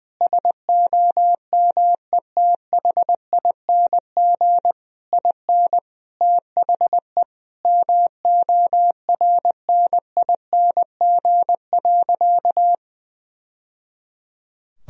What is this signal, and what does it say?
Morse code, heard as a keyed tone, about 19 wpm, 700 Hz. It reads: SOMETHING IN THE MORNING.
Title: Sentences from Top 1000 Words - Review Part 4 - 20wpm